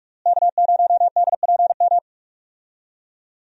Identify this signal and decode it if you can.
Morse code, heard as a keyed tone, about 42 wpm, 700 Hz. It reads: K0DPM
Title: USA Call Signs - 5 Characters - Repeated 2x - 45wpm